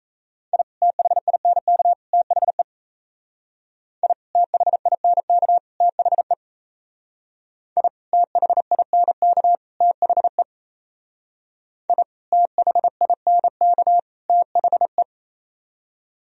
Morse code, heard as a keyed tone, about 35 wpm, 700 Hz. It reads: I THINK THE I THINK THE I THINK THE I THINK THE
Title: Speed-Racing - Sets of 3 Words 28wpm